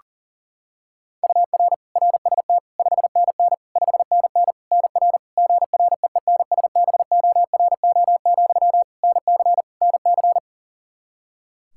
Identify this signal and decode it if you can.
Morse code, heard as a keyed tone, about 37 wpm, 700 Hz. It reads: UR RST 5NN 5NN NR GREENSBORO, NC NC
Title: Practice QSO - Exchange 1 - Beginning - 40wpm